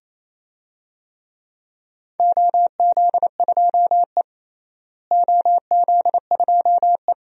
Morse code, heard as a keyed tone, about 28 wpm, 700 Hz. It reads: OZ2E OZ2E
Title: DXCC-100 Call Signs - 4 Characters - Repeated 2x - 28wpm